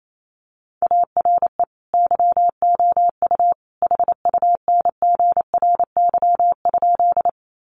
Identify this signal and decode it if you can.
Morse code, heard as a keyed tone, about 28 wpm, 700 Hz. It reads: ARE YOU HUNGRY?
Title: Sentences from Top 1400 Words 28wpm